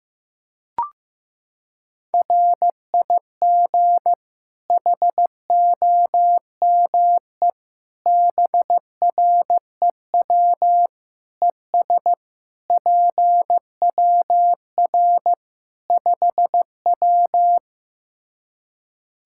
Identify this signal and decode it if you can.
Morse code, heard as a keyed tone, about 16 wpm, 700 Hz. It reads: RIG HOME BREW ES PWR 5W
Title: Practice QSO - Exchange 2 - Info - 15wpm